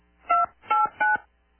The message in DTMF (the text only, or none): 289